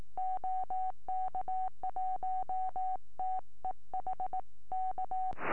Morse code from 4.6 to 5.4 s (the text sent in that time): K